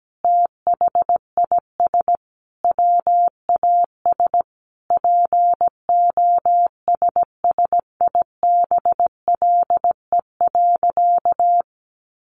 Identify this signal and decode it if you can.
Morse code, heard as a keyed tone, about 17 wpm, 700 Hz. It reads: THIS WAS POSSIBLE.
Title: Sentences from Top 500 Words 17wpm